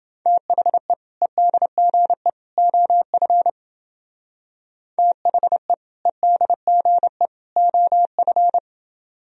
Morse code, heard as a keyed tone, about 28 wpm, 700 Hz. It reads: THE EDGE OF THE EDGE OF